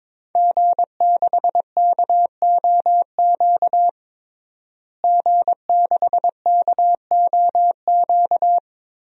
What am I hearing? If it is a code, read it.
Morse code, heard as a keyed tone, about 22 wpm, 700 Hz. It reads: G6KOQ G6KOQ
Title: DXCC-100 Call Signs - 5 Characters - Repeated 2x - 22wpm